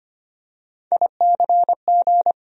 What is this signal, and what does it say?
Morse code, heard as a keyed tone, about 25 wpm, 700 Hz. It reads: ICG
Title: USA Call Sign Postfixes - 3 Letters 25wpm